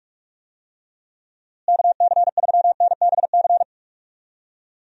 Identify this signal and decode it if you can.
Morse code, heard as a keyed tone, about 42 wpm, 700 Hz. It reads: KC3NBC
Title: USA Call Signs - 6 Characters - Repeated 2x - 45wpm